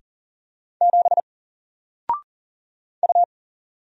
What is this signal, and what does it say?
Morse code, heard as a keyed tone, about 40 wpm, 700 Hz. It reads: 7 U
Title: Single Character (Letter:Number:Punctuation:Prosign) Alt 40wpm